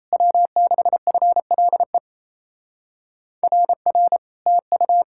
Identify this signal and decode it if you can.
Morse code, heard as a keyed tone, about 37 wpm, 700 Hz. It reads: W6FLE RR TU